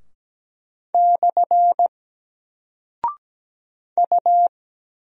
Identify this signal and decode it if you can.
Morse code, heard as a keyed tone, about 17 wpm, 700 Hz. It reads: / U
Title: Single Character (Letter:Number:Punctuation:Prosign) 17wpm